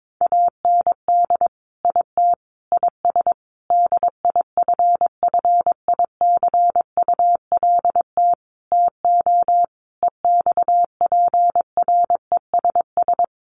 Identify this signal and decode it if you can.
Morse code, heard as a keyed tone, about 22 wpm, 700 Hz. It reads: AND IT IS DIFFICULT TO EXPRESS
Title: Sentences from Top 1100 Words 22wpm